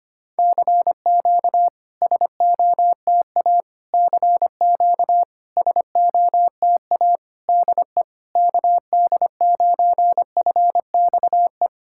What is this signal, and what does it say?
Morse code, heard as a keyed tone, about 25 wpm, 690 Hz. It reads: CQ SOTA CQ SOTA DE KD9FXE
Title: SOTA QSO Practice 25wpm